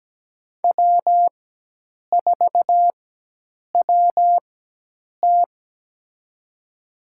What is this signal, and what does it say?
Morse code, heard as a keyed tone, about 17 wpm, spaced as for 8 wpm, 700 Hz. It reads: W4WT
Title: USA Call Signs - 4x Character Spacing - 17wpm